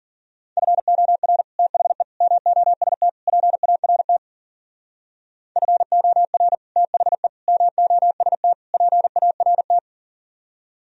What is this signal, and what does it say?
Morse code, heard as a keyed tone, about 51 wpm, 700 Hz. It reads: FOR THE MOST PART FOR THE MOST PART